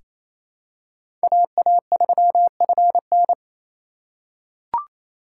Morse code, heard as a keyed tone, about 28 wpm, 700 Hz. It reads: AA3FN
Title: USA Call Signs - 5 Characters - 6x Character Spacing - 28wpm